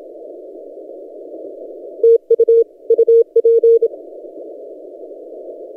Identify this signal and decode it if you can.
Morse code, heard as a keyed tone, about 27 wpm, 460 Hz. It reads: TU UP
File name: posts / t32rc-30cw